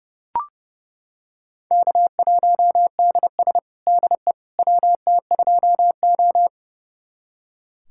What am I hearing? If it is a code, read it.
Morse code, heard as a keyed tone, about 29 wpm, 700 Hz. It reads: K1DS DE WT2O
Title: USA Callsign DE Callsign - 4 Characters - 30wpm